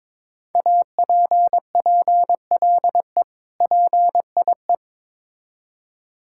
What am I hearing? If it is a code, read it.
Morse code, heard as a keyed tone, about 22 wpm, 700 Hz. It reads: APPLE PIE
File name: Difficult Words with Hints - 22wpm